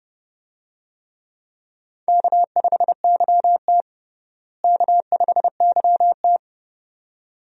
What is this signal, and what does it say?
Morse code, heard as a keyed tone, about 30 wpm, 700 Hz. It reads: K5YT K5YT
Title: USA Call Signs - 4 Characters - Repeated 2x - 30wpm